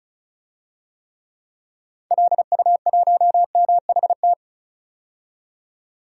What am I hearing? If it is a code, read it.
Morse code, heard as a keyed tone, about 35 wpm, 700 Hz. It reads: LU1MHT
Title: DXCC-100 Call Signs - 6 Characters - 35wpm